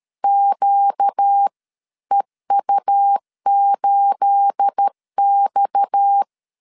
Morse code, heard as a keyed tone, about 13 wpm, spaced as for 11 wpm, 800 Hz. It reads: QEU8X
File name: CW / DET / 1 9 1500C S B 12wpm 3819